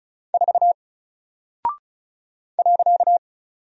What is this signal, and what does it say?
Morse code, heard as a keyed tone, about 36 wpm, 700 Hz. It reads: 4 .
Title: Single Character (Letter:Number:Punctuation:Prosign) Alt 35wpm